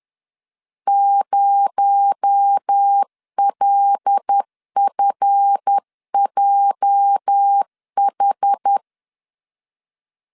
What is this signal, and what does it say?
Morse code, heard as a keyed tone, about 10 wpm, 800 Hz. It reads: 0LFJH
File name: LEZIONI / 8 YJ45 / 8 YJ 10wpm 2651